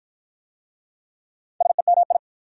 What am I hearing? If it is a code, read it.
Morse code, heard as a keyed tone, about 45 wpm, 690 Hz. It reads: SEMI